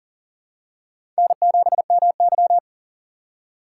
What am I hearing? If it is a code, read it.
Morse code, heard as a keyed tone, about 38 wpm, 700 Hz. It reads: N7MY
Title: USA Call Signs - 4 Characters - 3x Character Spacing - 40wpm